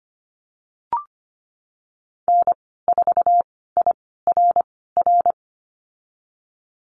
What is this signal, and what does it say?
Morse code, heard as a keyed tone, about 25 wpm, spaced as for 16 wpm, 700 Hz. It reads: N4IRR
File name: Lesson 15g - Number 5 - Call Signs 25wpm